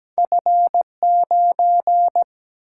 Morse code, heard as a keyed tone, about 17 wpm, 700 Hz. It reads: F9
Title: Single-Letter Single-Number 17wpm